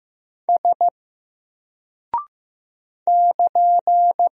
Morse code, heard as a keyed tone, about 15 wpm, 700 Hz. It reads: S <KN>
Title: Single Character (Letter:Number:Punctuation:Prosign) Alt 15wpm